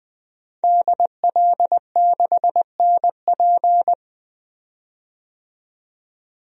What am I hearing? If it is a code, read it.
Morse code, heard as a keyed tone, about 21 wpm, 700 Hz. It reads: DL6NP
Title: DXCC-100 Call Signs - 5 Characters - 20wpm